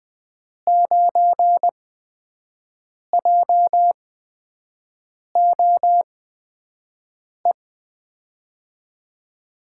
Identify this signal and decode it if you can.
Morse code, heard as a keyed tone, about 20 wpm, spaced as for 5 wpm, 700 Hz. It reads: 9JOE
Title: USA Call Signs - 8x Character Spacing - 20wpm